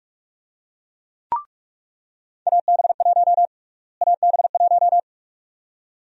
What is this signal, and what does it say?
Morse code, heard as a keyed tone, about 41 wpm, 700 Hz. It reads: AB1 AB1
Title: USA Call Sign Prefixes - Repeated 2x - 45wpm